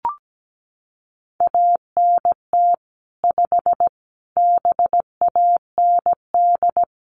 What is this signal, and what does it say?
Morse code, heard as a keyed tone, about 17 wpm, 700 Hz. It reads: ANT 5 BAND
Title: Common QSO Phrases 17wpm